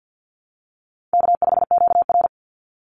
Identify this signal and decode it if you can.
Morse code, heard as a keyed tone, about 48 wpm, 700 Hz. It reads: K5QR
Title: NA SOTA Chaser Callsigns 50wpm